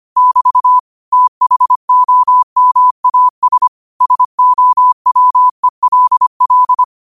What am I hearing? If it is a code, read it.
Morse code, heard as a keyed tone, about 25 wpm, 1000 Hz. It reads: <BT> THOMAS SOWELL
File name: qotdfiles / 25wpm